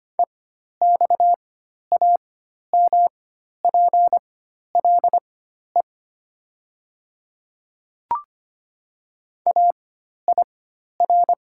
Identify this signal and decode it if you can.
Morse code, heard as a keyed tone, about 25 wpm, spaced as for 12 wpm, 700 Hz. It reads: EXAMPLE AIR
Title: 4x Character Spacing - Top 500 Words - 25wpm